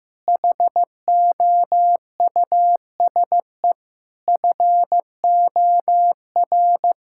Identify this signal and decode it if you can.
Morse code, heard as a keyed tone, about 15 wpm, 700 Hz. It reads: HOUSE FOR
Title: Sentences from Top 1000 Words - Review Part 5 - 15wpm